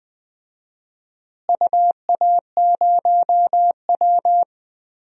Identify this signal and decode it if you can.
Morse code, heard as a keyed tone, about 20 wpm, 700 Hz. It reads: UA0W
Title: DXCC-100 Call Signs - 4 Characters - Repeated 2x - 20wpm